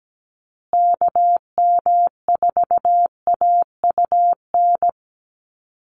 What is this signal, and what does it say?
Morse code, heard as a keyed tone, about 17 wpm, 700 Hz. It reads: KM4AUN
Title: USA Call Signs Repeated 2x 17wpm